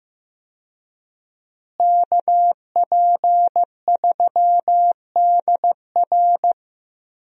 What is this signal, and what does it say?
Morse code, heard as a keyed tone, about 15 wpm, 700 Hz. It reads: KP3DR
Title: USA Call Signs - 5 Characters - Repeated 2x - 15wpm